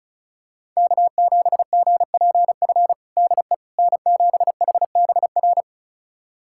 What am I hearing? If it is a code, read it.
Morse code, heard as a keyed tone, about 36 wpm, 700 Hz. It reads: K7GPF DE N7HBR
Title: USA Callsign DE Callsign - 5 Characters - 35wpm